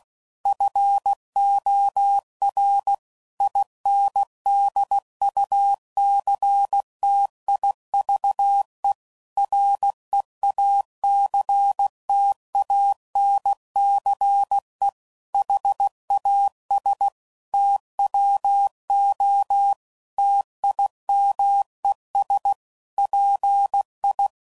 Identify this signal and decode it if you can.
Morse code, heard as a keyed tone, about 16 wpm, 800 Hz. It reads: FOR INDUCTIVE REACTANCE HAS TWO TIMES PI
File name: CW Code Groups @15WPM / Technical Subjects / XL